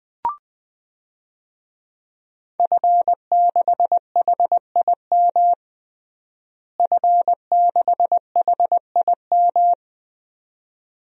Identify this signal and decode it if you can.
Morse code, heard as a keyed tone, about 21 wpm, 700 Hz. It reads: F6HIM F6HIM
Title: DXCC-100 Call Signs - 5 Characters - Repeated 2x - 20wpm